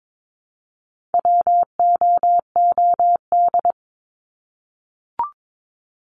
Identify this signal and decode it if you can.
Morse code, heard as a keyed tone, about 22 wpm, 700 Hz. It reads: WOOD
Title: Top 300 Words 22wpm